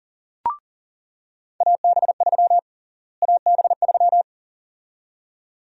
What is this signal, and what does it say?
Morse code, heard as a keyed tone, about 40 wpm, 700 Hz. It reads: AB3 AB3
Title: USA Call Sign Prefixes - Repeated 2x - 40wpm